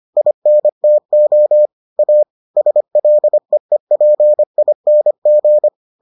Morse code, heard as a keyed tone, about 25 wpm, 600 Hz. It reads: INTO A SLEEPING